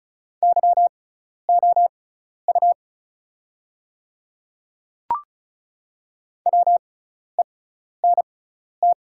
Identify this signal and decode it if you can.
Morse code, heard as a keyed tone, about 36 wpm, spaced as for 12 wpm, 690 Hz. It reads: YOU WENT
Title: 6x Character Spacing - Top 200 Words - 35wpm